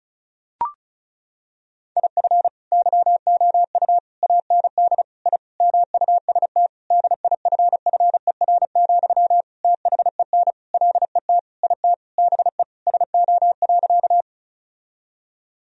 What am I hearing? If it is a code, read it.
Morse code, heard as a keyed tone, about 36 wpm, 700 Hz. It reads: IF YOU AND I MUST DIFFER, THEN LET IT BE SO.
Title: Sentences from Top 1000 Words - Review Part 1 - 35wpm